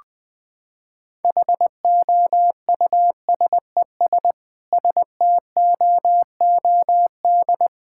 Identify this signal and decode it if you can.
Morse code, heard as a keyed tone, about 20 wpm, 700 Hz. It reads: HOUSES STOOD
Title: Sentences with 5-Word Phrases - 20wpm